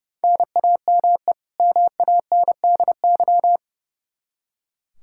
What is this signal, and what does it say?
Morse code, heard as a keyed tone, about 30 wpm, 700 Hz. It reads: NAME MANDY
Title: Practice QSO - Exchange 1 - Beginning - 30wpm